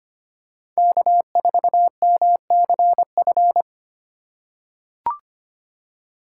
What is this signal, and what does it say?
Morse code, heard as a keyed tone, about 25 wpm, 700 Hz. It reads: K4MCF
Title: USA Call Signs - 5 Characters - 3x Character Spacing - 25wpm